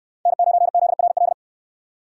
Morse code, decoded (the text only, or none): N9ZRL